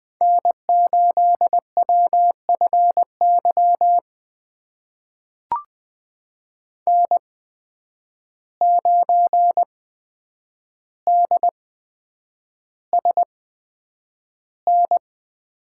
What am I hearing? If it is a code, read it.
Morse code, heard as a keyed tone, about 20 wpm, spaced as for 9 wpm, 700 Hz. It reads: N8WFY N9DSN
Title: USA Call Signs - 5 Characters - 8x Character Spacing - 20wpm